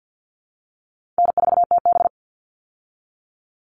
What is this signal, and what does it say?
Morse code, heard as a keyed tone, about 50 wpm, 700 Hz. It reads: N4TB
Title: Call Signs Repeated 2x - 50wpm